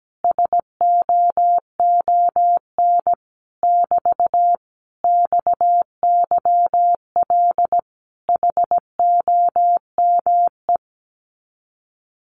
Morse code, heard as a keyed tone, about 17 wpm, 700 Hz. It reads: SOON <BT> XYL HOME
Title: Common QSO Phrases Encore 17wpm